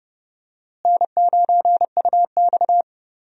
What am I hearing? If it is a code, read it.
Morse code, heard as a keyed tone, about 30 wpm, 700 Hz. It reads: N9UX
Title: USA Call Signs - 4 Characters - Repeated 3x - 30wpm